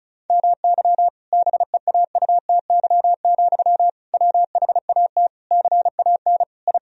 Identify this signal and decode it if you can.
Morse code, heard as a keyed tone, about 34 wpm, 700 Hz. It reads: MY BEAUTY, WHAT CAN I
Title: Sentences from Top 1000 Words - Review Part 2 - 35wpm